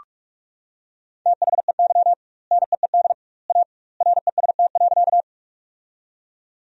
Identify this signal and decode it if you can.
Morse code, heard as a keyed tone, about 42 wpm, 700 Hz. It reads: THEY NEED A REST.
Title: 2x Word-Spacing - Sentences from Top 500 Words - 45wpm